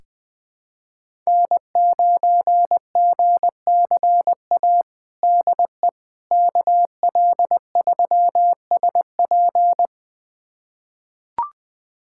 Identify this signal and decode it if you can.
Morse code, heard as a keyed tone, about 20 wpm, 700 Hz. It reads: N9GCA DE KL3SP
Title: USA Callsign DE Callsign - 5 Characters - 20wpm